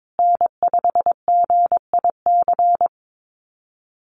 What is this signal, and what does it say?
Morse code, heard as a keyed tone, about 22 wpm, 700 Hz. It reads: N5GIC
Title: USA Call Signs Repeated 2x 22wpm